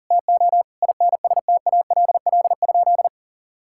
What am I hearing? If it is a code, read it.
Morse code, heard as a keyed tone, about 40 wpm, 700 Hz. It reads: TO INSTALL?
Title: Sentences from Top 1600 Words 40wpm